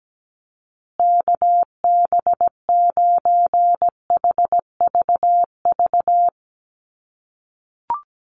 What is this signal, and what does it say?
Morse code, heard as a keyed tone, about 17 wpm, 700 Hz. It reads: KB9HVV
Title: USA Call Signs Repeated 3x 17wpm